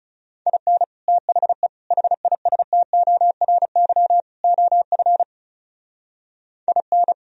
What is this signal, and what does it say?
Morse code, heard as a keyed tone, about 37 wpm, 700 Hz. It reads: IN THE HISTORY OF IN
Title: Speed-Racing - Sets of 4 Words 30wpm